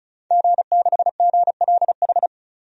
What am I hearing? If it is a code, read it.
Morse code, heard as a keyed tone, about 36 wpm, 700 Hz. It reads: G6GLH
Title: DXCC-100 Call Signs - 5 Characters - 35wpm